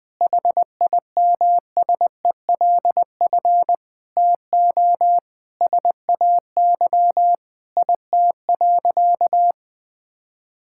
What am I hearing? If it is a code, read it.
Morse code, heard as a keyed tone, about 19 wpm, 700 Hz. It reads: HIMSELF TO SAY IT.
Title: Sentences from Top 1600 Words 20wpm